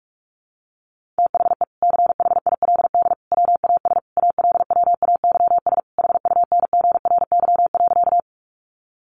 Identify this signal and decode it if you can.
Morse code, heard as a keyed tone, about 44 wpm, 700 Hz. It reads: THE CHILD WAS ALWAYS HUNGRY.
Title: Sentences from Top 1400 Words 45wpm